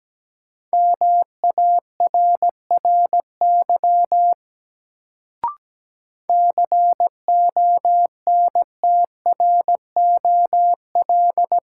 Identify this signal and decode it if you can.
Morse code, heard as a keyed tone, about 17 wpm, 700 Hz. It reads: MARRY CONTROL
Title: Sentences from Top 1200 Words - New Word Drill Alt - 17wpm